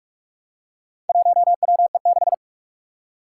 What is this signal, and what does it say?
Morse code, heard as a keyed tone, about 44 wpm, 700 Hz. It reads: 1WE6
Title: Four Challenging Letter-Number Combinations 45wpm